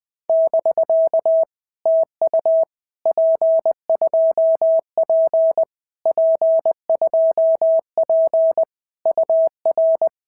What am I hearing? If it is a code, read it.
Morse code, heard as a keyed tone, about 20 wpm, 650 Hz. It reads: <BK> TU P2P P2P UR
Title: POTA QSO Practice - 20wpm